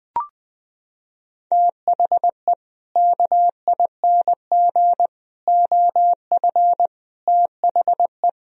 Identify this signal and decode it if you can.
Morse code, heard as a keyed tone, about 21 wpm, 700 Hz. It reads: THE KING OF THE
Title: Sentences from Top 1000 Words - Review Part 2 - 20wpm